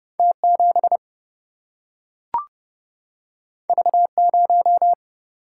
Morse code, heard as a keyed tone, about 29 wpm, 700 Hz. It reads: T7 V0
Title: Single-Letter Single-Number 30wpm